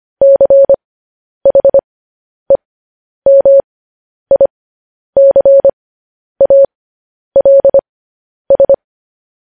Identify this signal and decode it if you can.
Morse code, heard as a keyed tone, about 25 wpm, spaced as for 10 wpm, 550 Hz. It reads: CHEMICALS
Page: News Headlines in Morse Code at 10 WPM